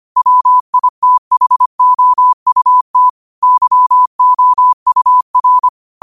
Morse code, heard as a keyed tone, about 25 wpm, 1000 Hz. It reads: WITHOUT YOUR
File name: 25wpm